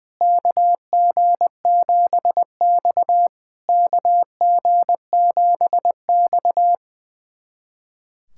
Morse code, heard as a keyed tone, about 20 wpm, 700 Hz. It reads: KG7X KG7X
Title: SOTA QSO Practice 20wpm